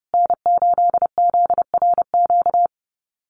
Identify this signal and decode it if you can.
Morse code, heard as a keyed tone, about 30 wpm, 700 Hz. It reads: N8ZRQ
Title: USA Call Signs Repeated 3x 30wpm